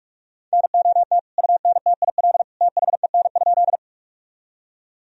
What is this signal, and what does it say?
Morse code, heard as a keyed tone, about 42 wpm, 700 Hz. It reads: NOT UNTIL THEN?
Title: Sentences from Top 1000 Words - Review Part 1 - 45wpm